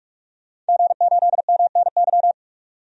Morse code, heard as a keyed tone, about 46 wpm, 700 Hz. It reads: G8MNY
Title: DXCC-100 Call Signs - 5 Characters - 45wpm